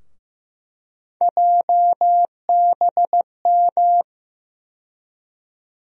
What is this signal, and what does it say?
Morse code, heard as a keyed tone, about 15 wpm, 700 Hz. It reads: JBM
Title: USA Call Sign Postfixes - 3 Letters 15wpm